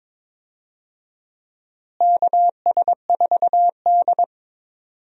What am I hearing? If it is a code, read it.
Morse code, heard as a keyed tone, about 22 wpm, 700 Hz. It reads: KS4D
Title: USA Call Signs - 4 Characters - Repeated 3x - 22wpm